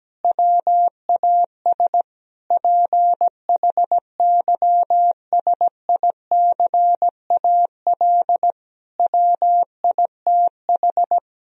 Morse code, heard as a keyed tone, about 17 wpm, 700 Hz. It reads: WAS PHYSICAL WITH
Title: Sentences from Top 1000 Words - Review Part 4 - 17wpm